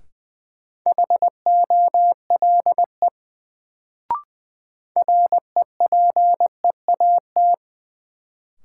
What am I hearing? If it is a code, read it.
Morse code, heard as a keyed tone, about 20 wpm, 700 Hz. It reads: HOLE REPEAT
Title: Sentences from Top 1600 Words - New Word Drill - 20wpm